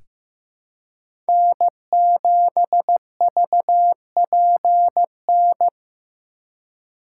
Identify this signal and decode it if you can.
Morse code, heard as a keyed tone, about 15 wpm, 700 Hz. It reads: N7VPN
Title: USA Call Signs - 5 Characters - 3x Character Spacing - 15wpm